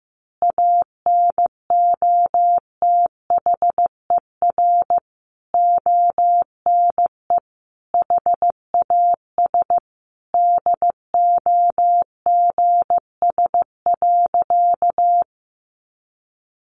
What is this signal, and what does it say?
Morse code, heard as a keyed tone, about 15 wpm, 700 Hz. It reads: ANOTHER ONE HAS DOGS.
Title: Additional Easy Phrases 15wpm